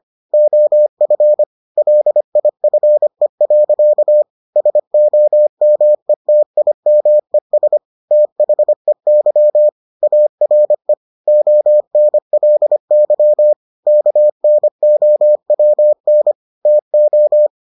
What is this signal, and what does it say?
Morse code, heard as a keyed tone, about 25 wpm, 600 Hz. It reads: OF LIFE. SOMETIMES THEY ARE ONLY KNOWN TO